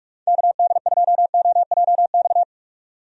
Morse code, heard as a keyed tone, about 43 wpm, 700 Hz. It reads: KD2OJX
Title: USA Call Signs - 6 Characters - 45wpm